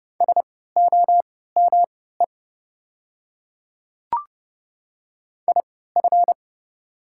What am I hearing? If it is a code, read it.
Morse code, heard as a keyed tone, about 30 wpm, spaced as for 17 wpm, 700 Hz. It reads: SOME IF
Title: 3x Character Spacing - Top 100 Words - 30wpm